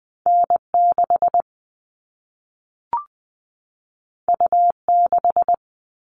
Morse code, heard as a keyed tone, about 20 wpm, 700 Hz. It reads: N 6 U 6
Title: All Cut Numbers - 2-Digit 20wpm